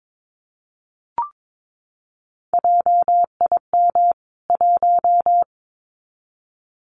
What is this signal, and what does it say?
Morse code, heard as a keyed tone, about 22 wpm, 700 Hz. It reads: JIM 1
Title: CWT Exchange 22wpm